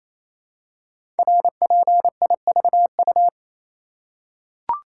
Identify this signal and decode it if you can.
Morse code, heard as a keyed tone, about 27 wpm, 700 Hz. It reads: RPIVU
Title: Code Groups - 5 Letters - 28wpm